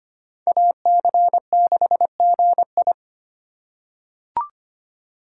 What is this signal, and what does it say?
Morse code, heard as a keyed tone, about 25 wpm, 700 Hz. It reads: AC6GI
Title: USA Call Signs - 5 Characters - 3x Character Spacing - 25wpm